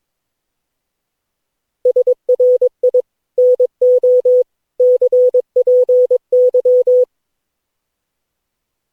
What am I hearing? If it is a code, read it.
Morse code, heard as a keyed tone, about 22 wpm, 500 Hz. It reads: SRI NO CPY
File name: cw / Session9 cw training